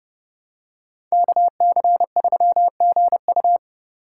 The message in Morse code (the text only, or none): KC3GU